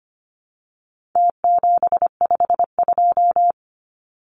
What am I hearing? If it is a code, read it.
Morse code, heard as a keyed tone, about 25 wpm, 700 Hz. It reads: T752
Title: Numbers - Contest Style - 4-Digit 25wpm